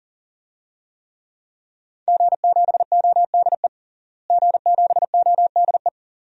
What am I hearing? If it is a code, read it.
Morse code, heard as a keyed tone, about 40 wpm, 700 Hz. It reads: G7ODE G7ODE
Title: DXCC-100 Call Signs - 5 Characters - Repeated 2x - 40wpm